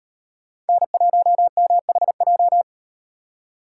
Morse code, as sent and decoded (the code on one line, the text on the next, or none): -. .---- -- .... .---
N1MHJ